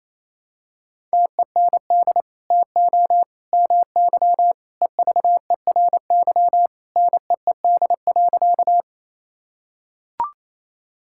Morse code, heard as a keyed tone, about 28 wpm, 700 Hz. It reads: TEND TO MY EVERY NEED.
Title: Sentences from Top 1000 Words - Review Part 5 - 28wpm